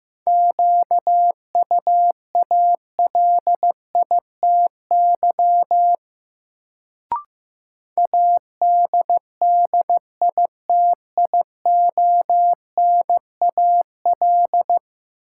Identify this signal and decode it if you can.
Morse code, heard as a keyed tone, about 16 wpm, 700 Hz. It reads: QUALITY ADDITIONAL
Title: Sentences from Top 1000 Words - Review Part 4 - New Word Drill - 15wpm